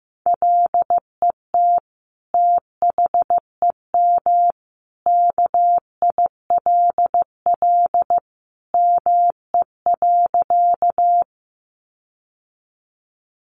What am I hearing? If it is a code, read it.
Morse code, heard as a keyed tone, about 15 wpm, 700 Hz. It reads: LET THEM KILL ME.
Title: Sentences from Top 1400 Words 15wpm